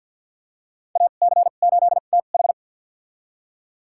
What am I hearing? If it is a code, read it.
Morse code, heard as a keyed tone, about 52 wpm, spaced as for 37 wpm, 700 Hz. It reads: AC8TH
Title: USA Call Signs - 5 Characters - 2x Character Spacing - 50wpm